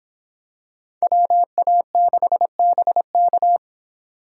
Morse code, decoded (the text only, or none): WA6BK